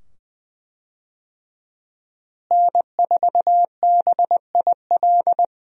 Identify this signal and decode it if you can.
Morse code, heard as a keyed tone, about 20 wpm, 700 Hz. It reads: N4BIL
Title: USA Call Signs - 5 Characters - Repeated 2x - 20wpm